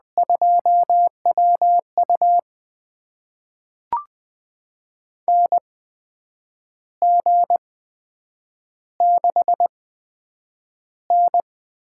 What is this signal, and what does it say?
Morse code, heard as a keyed tone, about 20 wpm, spaced as for 8 wpm, 700 Hz. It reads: 2WU NG6N
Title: USA Call Signs - 8x Character Spacing - 20wpm